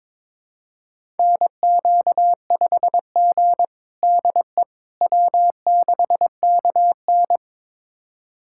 Morse code, heard as a keyed tone, about 22 wpm, 700 Hz. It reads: NQ5G DE W6KN
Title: USA Callsign DE Callsign - 4 Characters - 22wpm